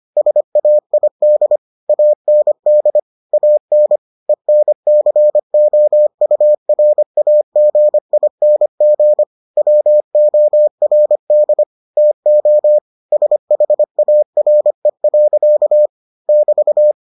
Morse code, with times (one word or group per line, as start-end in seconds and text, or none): 0.2-1.6 SAID
1.9-3.0 AND
3.3-4.0 AN
4.3-9.2 ENCOURAGING
9.6-11.6 WORD
12.0-12.8 TO
13.1-15.9 SHARE.
16.3-16.9 <BT>